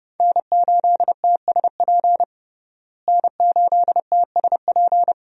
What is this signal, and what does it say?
Morse code, heard as a keyed tone, about 30 wpm, 700 Hz. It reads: N8TSP N8TSP
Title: USA Call Signs - 5 Characters - Repeated 2x - 30wpm